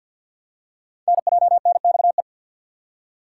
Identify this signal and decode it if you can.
Morse code, heard as a keyed tone, about 47 wpm, 700 Hz. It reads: NJNXE
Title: Code Groups - 5 Letters - 50wpm